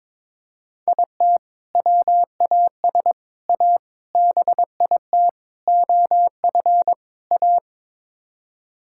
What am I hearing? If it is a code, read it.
Morse code, heard as a keyed tone, about 22 wpm, 700 Hz. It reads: IT WAS A BIT OF A
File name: practice / 22wpm / Sets of 6 Words - 22wpm